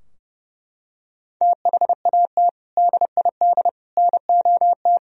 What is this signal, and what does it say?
Morse code, heard as a keyed tone, about 31 wpm, 700 Hz. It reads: THAT DID NOT
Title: Sentences from Top 1000 Words - Review Part 2 - 30wpm